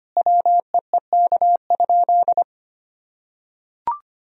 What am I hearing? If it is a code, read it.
Morse code, heard as a keyed tone, about 25 wpm, 700 Hz. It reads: WEEK?
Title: Sentences from Top 1000 Words - Review Part 2 - 25wpm